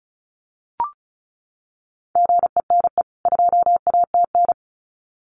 Morse code, heard as a keyed tone, about 34 wpm, 700 Hz. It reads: GENE 2ATN
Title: CWT Exchange 35wpm